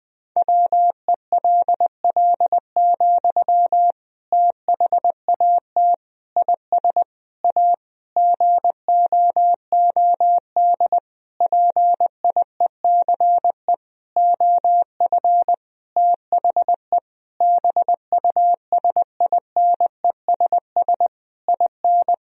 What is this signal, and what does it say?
Morse code, heard as a keyed tone, about 20 wpm, 700 Hz. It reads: WELL, THAT IS A GOOD PIECE OF THE BUSINESS IN
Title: Sentences from Top 1000 Words - Review Part 2 - 20wpm